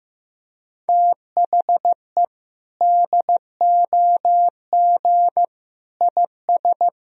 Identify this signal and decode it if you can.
Morse code, heard as a keyed tone, about 15 wpm, 700 Hz. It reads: THE DOG IS
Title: Sentences from Top 400 Words and Abbreviations 15wpm